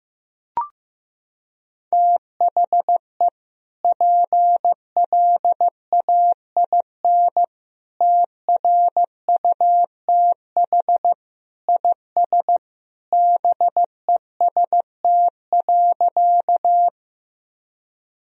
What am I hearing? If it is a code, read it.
Morse code, heard as a keyed tone, about 15 wpm, 700 Hz. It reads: THE PLAIN TRUTH IS BEST.
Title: Sentences from Top 1000 Words - Review Part 2 - 15wpm